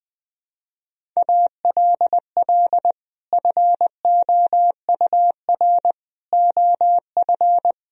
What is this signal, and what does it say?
Morse code, heard as a keyed tone, about 21 wpm, 700 Hz. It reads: ALL FOUR OF
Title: Sentences from Top 1000 Words - Review Part 1 - 20wpm